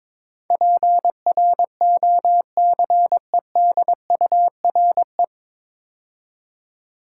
Morse code, heard as a keyed tone, about 22 wpm, 700 Hz. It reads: PROCEDURE
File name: 9 Letter Words - First 3 Letters Given - 22wpm